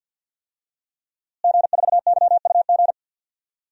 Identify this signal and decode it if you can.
Morse code, heard as a keyed tone, about 52 wpm, 700 Hz. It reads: G4YUG
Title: DXCC-100 Call Signs - 5 Characters - Repeated 3x - 50wpm